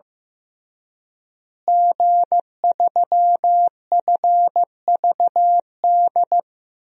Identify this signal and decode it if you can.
Morse code, heard as a keyed tone, about 15 wpm, 700 Hz. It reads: G3FVD
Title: DXCC-100 Call Signs - 5 Characters - Repeated 3x - 15wpm